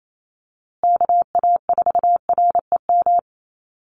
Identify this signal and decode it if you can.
Morse code, heard as a keyed tone, about 28 wpm, 700 Hz. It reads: KA4REM
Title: USA Call Signs Repeated 2x 28wpm